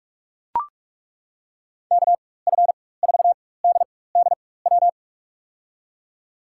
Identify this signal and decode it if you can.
Morse code, heard as a keyed tone, about 42 wpm, spaced as for 21 wpm, 700 Hz. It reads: KF4DDW
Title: USA Call Signs - 6 Characters - 4x Character Spacing - 45wpm